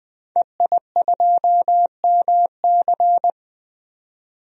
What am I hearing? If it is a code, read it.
Morse code, heard as a keyed tone, about 20 wpm, 700 Hz. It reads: EI2MC
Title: DXCC-100 Call Signs - 5 Characters - Repeated 2x - 20wpm